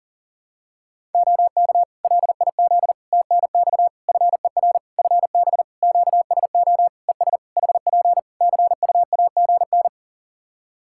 Morse code, heard as a keyed tone, about 40 wpm, 700 Hz. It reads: OK LIZ TNX FER FB QSO ES HP CUAGN
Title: K1USN SST QSO Practice 40wpm